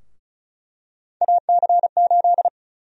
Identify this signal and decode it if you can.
Morse code, heard as a keyed tone, about 36 wpm, 700 Hz. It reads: AC8
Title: USA Call Sign Prefixes - Repeated 2x - 35wpm